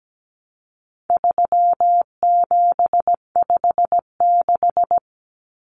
Morse code, heard as a keyed tone, about 17 wpm, 700 Hz. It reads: 3756
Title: Common Cut Numbers - 4-Digit 17wpm